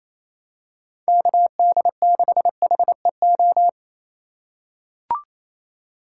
Morse code, heard as a keyed tone, about 28 wpm, 700 Hz. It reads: KD6HEO
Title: USA Call Signs - 4x Character Spacing - 28wpm